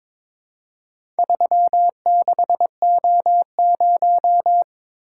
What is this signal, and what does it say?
Morse code, heard as a keyed tone, about 22 wpm, 700 Hz. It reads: 36O0
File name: practice / 22wpm / Four Letter-Number Combinations - 22wpm